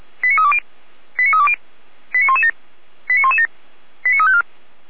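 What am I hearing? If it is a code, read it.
Five-tone selective call (ZVEI), heard as quick runs of five tones, tones of about 80 ms. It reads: 89320 89320 89117 89117 89354
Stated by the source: ZVEI five-tone selective call received by SDR, from the audio archive of Feuerwehr Orlamünde.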